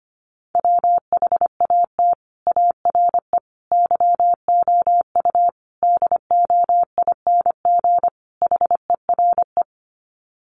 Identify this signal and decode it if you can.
Morse code, heard as a keyed tone, about 25 wpm, 700 Hz. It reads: WHAT ARE YOU DOING HERE
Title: Sets of 5 Words 25wpm